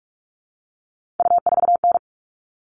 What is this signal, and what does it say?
Morse code, heard as a keyed tone, about 44 wpm, 700 Hz. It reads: U4N
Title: All Cut Numbers - 3-Digit 45wpm